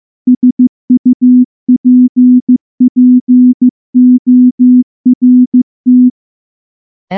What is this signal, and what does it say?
Morse code, heard as a keyed tone, about 14 wpm, 250 Hz. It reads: SUPPORT
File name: morse-training-15WPM